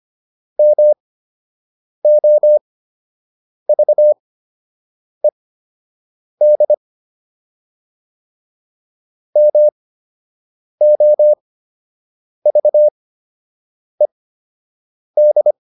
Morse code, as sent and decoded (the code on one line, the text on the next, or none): -- --- ...- . -.. / -- --- ...- . -..
MOVED MOVED